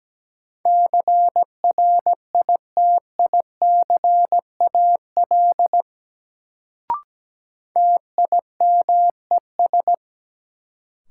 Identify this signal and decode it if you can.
Morse code, heard as a keyed tone, about 17 wpm, 700 Hz. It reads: CRITICAL TIMES
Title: Sentences from Top 1100 Words - New Word Drill Alt - 17wpm